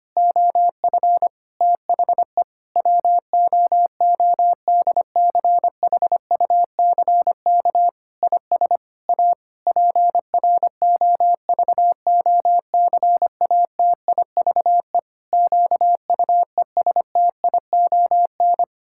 Morse code, heard as a keyed tone, about 25 wpm, 700 Hz. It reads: OF THE WOODCHUCK IS A PROVOCATIVE QUESTION
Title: Pangram Challenge v2 - 25wpm